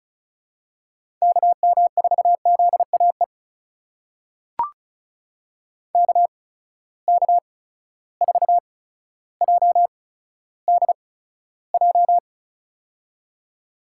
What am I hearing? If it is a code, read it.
Morse code, heard as a keyed tone, about 36 wpm, spaced as for 15 wpm, 700 Hz. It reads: KM4ZAE KK4JDJ
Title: USA Call Signs - 8x Character Spacing - 35wpm